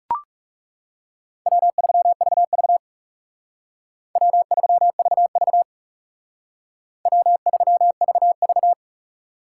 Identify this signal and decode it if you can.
Morse code, heard as a keyed tone, about 36 wpm, 700 Hz. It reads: W3VV W3VV W3VV
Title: Speed Racing - USA 1x2 Call Signs 30wpm